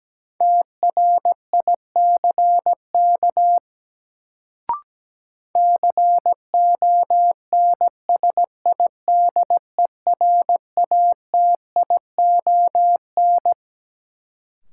Morse code, heard as a keyed tone, about 17 wpm, 700 Hz. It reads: TRICK CONSIDERATION
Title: Sentences from Top 1500 Words - New Word Drill - 17wpm